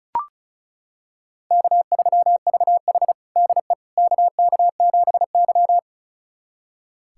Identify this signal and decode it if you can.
Morse code, heard as a keyed tone, about 36 wpm, 700 Hz. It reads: K3VH DE KK7Y
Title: USA Callsign DE Callsign - 4 Characters - 35wpm